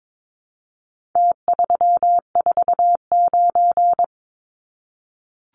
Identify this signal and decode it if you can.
Morse code, heard as a keyed tone, about 22 wpm, 700 Hz. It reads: T349
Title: Numbers - Contest Style - 4-Digit 22wpm